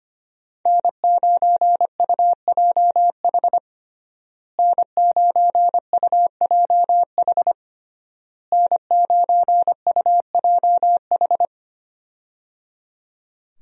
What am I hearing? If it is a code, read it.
Morse code, heard as a keyed tone, about 25 wpm, 700 Hz. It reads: N9UJH N9UJH N9UJH
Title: USA Call Signs - 5 Characters - Repeated 3x - 25wpm